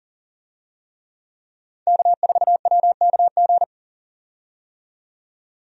Morse code, heard as a keyed tone, about 40 wpm, 700 Hz. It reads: K4WKG
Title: USA Call Signs - 5 Characters - 40wpm